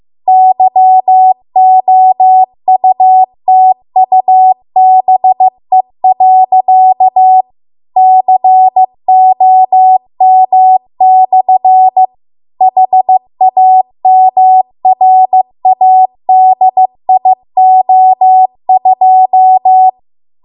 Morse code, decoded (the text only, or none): YOUTUBE.COM/HAMRADIO2